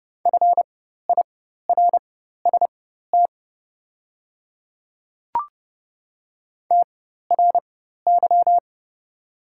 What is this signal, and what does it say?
Morse code, heard as a keyed tone, about 30 wpm, spaced as for 14 wpm, 700 Hz. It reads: FIRST TRY
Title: 4x Character Spacing - Top 200 Words - 30wpm